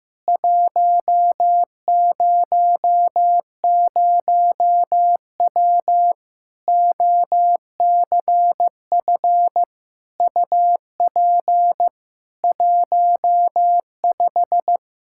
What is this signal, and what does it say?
Morse code, heard as a keyed tone, about 15 wpm, 700 Hz. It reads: 100W OCF UP 15
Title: K1USN SST QSO Practice 15wpm